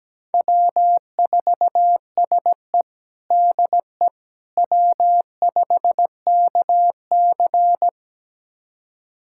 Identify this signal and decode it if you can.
Morse code, heard as a keyed tone, about 17 wpm, 700 Hz. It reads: W4SE DE W5KC
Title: USA Callsign DE Callsign - 4 Characters - 17wpm